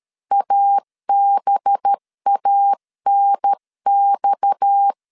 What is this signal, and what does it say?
Morse code, heard as a keyed tone, about 13 wpm, 800 Hz. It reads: ABANX
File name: CW / DET / 1 6 1500C S C 12wpm 3533